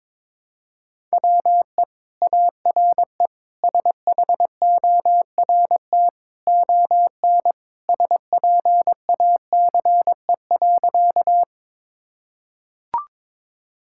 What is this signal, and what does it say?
Morse code, heard as a keyed tone, about 22 wpm, 700 Hz. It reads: WE ARE SHORT ON SPACE.
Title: Sentences from Top 1000 Words - Review Part 5 - 22wpm